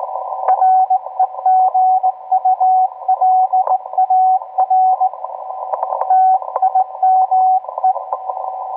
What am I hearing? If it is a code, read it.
Morse code, heard as a keyed tone, about 17 wpm, 750 Hz, with radio noise. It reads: REGULAR TIME